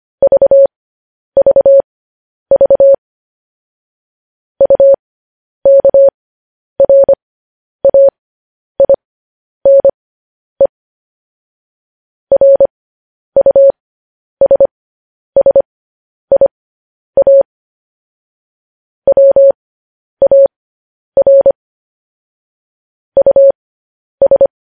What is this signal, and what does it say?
Morse code, heard as a keyed tone, about 25 wpm, spaced as for 10 wpm, 550 Hz. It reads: VVV UKRAINE RUSSIA WAR US
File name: podcast / News headlines-1x-25x10